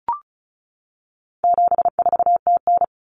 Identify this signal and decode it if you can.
Morse code, heard as a keyed tone, about 34 wpm, 700 Hz. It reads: 74TN
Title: Common Cut Numbers - 4-Digit 35wpm